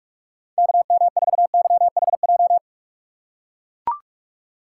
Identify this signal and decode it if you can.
Morse code, heard as a keyed tone, about 43 wpm, 700 Hz. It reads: KM4YHJ
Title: USA Call Signs - 3x Character Spacing - 45wpm